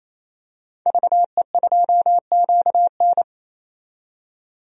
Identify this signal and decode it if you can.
Morse code, heard as a keyed tone, about 27 wpm, 700 Hz. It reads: VE2QN
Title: DXCC-100 Call Signs - 5 Characters - 28wpm